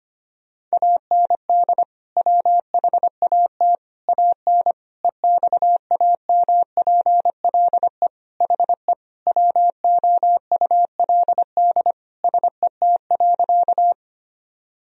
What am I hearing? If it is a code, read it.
Morse code, heard as a keyed tone, about 25 wpm, 700 Hz. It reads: AND WHAT AN EXAMPLE HE WOULD SET.
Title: Sentences from Top 1000 Words - Review Part 1 - 25wpm